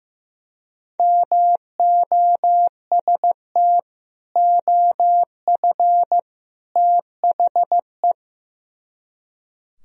Morse code, heard as a keyed tone, about 16 wpm, 700 Hz. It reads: MOST OF THE
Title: Speed-Racing - Sets of 3 Words 15wpm